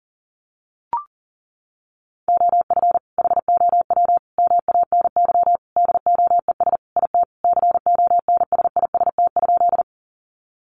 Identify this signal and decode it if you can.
Morse code, heard as a keyed tone, about 40 wpm, 700 Hz. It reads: OF HOW MANY DOES IT CONSIST?
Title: Sentences from Top 1400 Words 40wpm